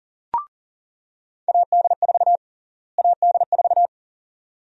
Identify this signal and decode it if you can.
Morse code, heard as a keyed tone, about 40 wpm, 700 Hz. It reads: AD4 AD4
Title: USA Call Sign Prefixes - Repeated 2x - 40wpm